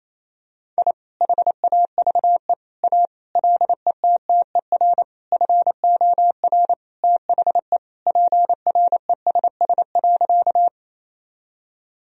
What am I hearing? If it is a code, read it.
Morse code, heard as a keyed tone, about 28 wpm, 700 Hz. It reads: I HAVE A LETTER FOR THE PRESS.
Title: Sentences from Top 1000 Words - Review Part 1 - 28wpm